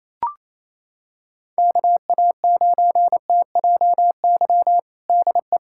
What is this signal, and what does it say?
Morse code, heard as a keyed tone, about 28 wpm, 700 Hz. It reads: KA9TJY DE
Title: USA Callsign DE Callsign 28wpm